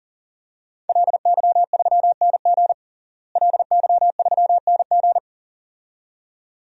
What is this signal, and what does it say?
Morse code, heard as a keyed tone, about 40 wpm, 700 Hz. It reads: LY3NG LY3NG
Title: DXCC-100 Call Signs - 5 Characters - Repeated 2x - 40wpm